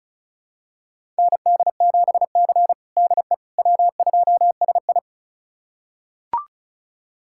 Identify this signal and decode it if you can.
Morse code, heard as a keyed tone, about 36 wpm, 700 Hz. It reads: ND7C DE W2SI
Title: USA Callsign DE Callsign - 4 Characters - 35wpm